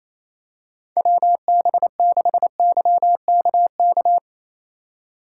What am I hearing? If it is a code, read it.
Morse code, heard as a keyed tone, about 28 wpm, 700 Hz. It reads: WB6YKK
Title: USA Call Signs - 6 Characters - 4x Character Spacing - 28wpm